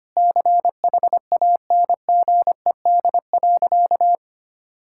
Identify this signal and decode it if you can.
Morse code, heard as a keyed tone, about 25 wpm, 700 Hz. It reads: CHANGED.
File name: practice / 25wpm / Sentences with 4-Word Phrases - 25wpm